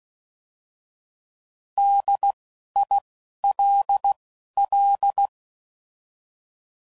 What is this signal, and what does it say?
Morse code, heard as a keyed tone, about 16 wpm, spaced as for 11 wpm, 800 Hz. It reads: DILL
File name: Prefixes / By First Letter / Ds